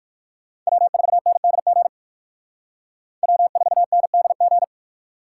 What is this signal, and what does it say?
Morse code, heard as a keyed tone, about 57 wpm, 700 Hz. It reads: W4NDG W4NDG